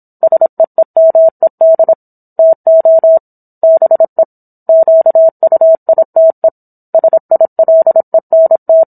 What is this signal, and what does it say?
Morse code, heard as a keyed tone, about 27 wpm, 650 Hz. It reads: SEEMED TO BE QUITE SILENT